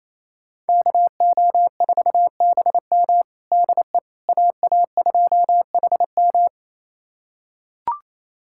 Morse code, heard as a keyed tone, about 28 wpm, 700 Hz. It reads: KO4BM DE AA2HM
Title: USA Callsign DE Callsign - 5 Characters - 28wpm